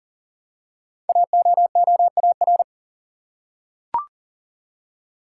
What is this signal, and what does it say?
Morse code, heard as a keyed tone, about 42 wpm, 700 Hz. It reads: AOOAR
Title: Code Groups - 5 Letter:Numbers - 40wpm